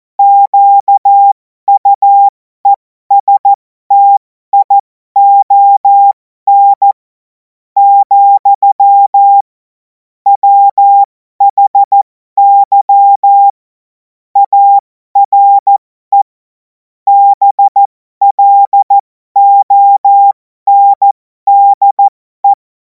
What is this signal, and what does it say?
Morse code, heard as a keyed tone, about 14 wpm, spaced as for 12 wpm, 800 Hz. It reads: QUESTION, WHY ARE BLONDE
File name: Morsekurs / Witze